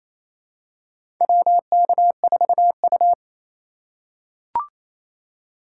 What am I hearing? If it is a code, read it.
Morse code, heard as a keyed tone, about 28 wpm, 700 Hz. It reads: WK4U
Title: USA Call Signs - 3x Character Spacing - 28wpm